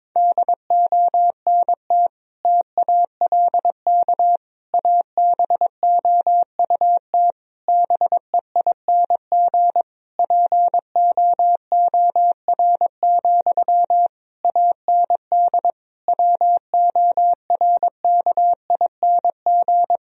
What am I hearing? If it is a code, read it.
Morse code, heard as a keyed tone, about 22 wpm, 700 Hz. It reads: DONT TALK ABOUT BEING POOR, AND WORKING
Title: Sentences from Top 1000 Words - Review Part 4 - 22wpm